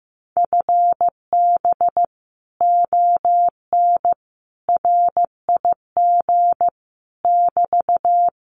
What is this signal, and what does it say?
Morse code, heard as a keyed tone, about 15 wpm, 700 Hz. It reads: FB ON RIG <BT>
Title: Common QSO Phrases Encore 15wpm